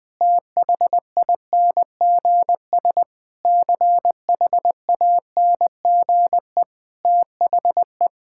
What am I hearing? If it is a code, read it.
Morse code, heard as a keyed tone, about 20 wpm, 700 Hz. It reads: THINGS CHANGE THE
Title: Sentences with 9-Word Phrases - 20wpm